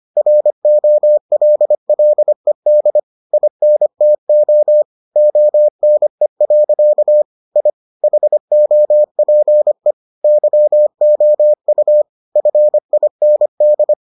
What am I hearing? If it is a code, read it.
Morse code, heard as a keyed tone, about 25 wpm, 600 Hz. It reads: ROLLED INTO ONE. I HOPE YOU FIND